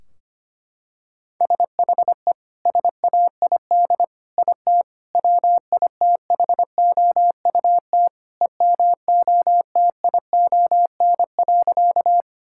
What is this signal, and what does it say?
Morse code, heard as a keyed tone, about 25 wpm, 700 Hz. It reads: SHE SAID IT WITHOUT EMOTION.
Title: Sentences from Top 1600 Words 25wpm